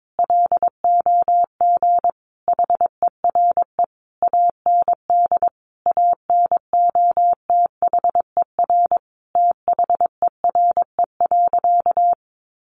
Morse code, heard as a keyed tone, about 22 wpm, 700 Hz. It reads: LOG HERE AND ANOTHER THERE.
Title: Sentences from Top 1400 Words 22wpm